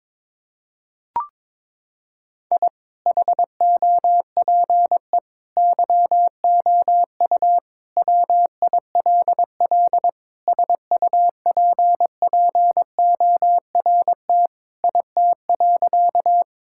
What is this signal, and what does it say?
Morse code, heard as a keyed tone, about 21 wpm, 700 Hz. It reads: I HOPE YOU WILL SUPPORT IT.
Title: Sentences from Top 1000 Words - Review Part 5 - 22wpm